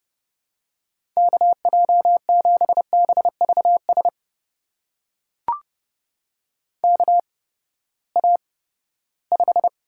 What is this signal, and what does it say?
Morse code, heard as a keyed tone, about 30 wpm, spaced as for 17 wpm, 700 Hz. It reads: KJ7BVS KA5
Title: USA Call Signs - 6 Characters - 8x Character Spacing - 30wpm